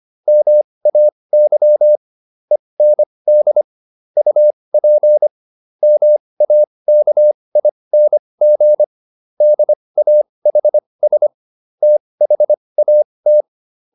Morse code, heard as a keyed tone, about 25 wpm, spaced as for 20 wpm, 600 Hz. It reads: MAY END UP MAKING DAHS THAT